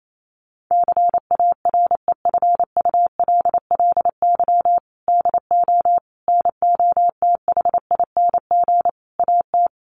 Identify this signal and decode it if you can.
Morse code, heard as a keyed tone, about 28 wpm, 700 Hz. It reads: CAREFULLY DO NOTHING AT
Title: Sentences from Top 1300 Words 28wpm